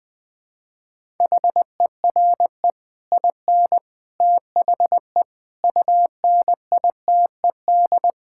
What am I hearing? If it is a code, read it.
Morse code, heard as a keyed tone, about 20 wpm, 700 Hz. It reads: HERE IN THE UNITED
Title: Sentences with 5-Word Phrases - Spoken Beginning - 20wpm